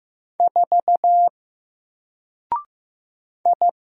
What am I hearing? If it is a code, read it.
Morse code, heard as a keyed tone, about 15 wpm, 700 Hz. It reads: 4 I
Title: Single Letter-Number 15wpm